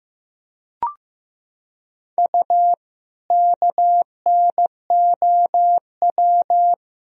Morse code, heard as a keyed tone, about 15 wpm, 700 Hz. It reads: U KNOW
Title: Sentences from Top 100 Words and Abbreviations 15wpm